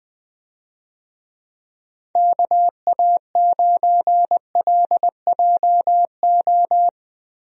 Morse code, heard as a keyed tone, about 20 wpm, 700 Hz. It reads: KA9LJO
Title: USA Call Signs - 6 Characters - Repeated 2x - 20wpm